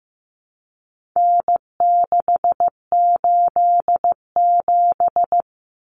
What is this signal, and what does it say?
Morse code, heard as a keyed tone, about 15 wpm, 700 Hz. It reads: N687
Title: Common Cut Numbers - 4-Digit 15wpm